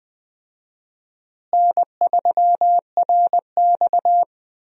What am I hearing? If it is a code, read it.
Morse code, heard as a keyed tone, about 20 wpm, 700 Hz. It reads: N3RX
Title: USA Call Signs - 4 Characters - Repeated 3x - 20wpm